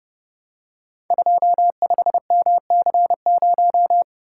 Morse code, heard as a keyed tone, about 30 wpm, 700 Hz. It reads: 25MC0
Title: Code Groups - 5 Letter:Numbers - 30wpm